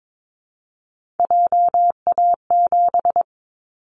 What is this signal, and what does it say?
Morse code, heard as a keyed tone, about 22 wpm, 700 Hz. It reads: JA7
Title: Callsign Prefixes for Top 50 DXCC - 22wpm